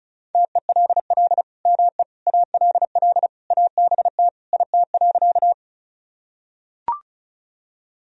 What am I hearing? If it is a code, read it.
Morse code, heard as a keyed tone, about 36 wpm, 700 Hz. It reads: TELL ME ALL ABT IT.
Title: Sentences from Top 200 Words and Abbreviations 35wpm